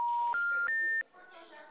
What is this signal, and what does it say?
Special information tone, with speech.
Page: Cordeiropolis